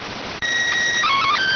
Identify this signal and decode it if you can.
SSTV (slow-scan television), PD 290.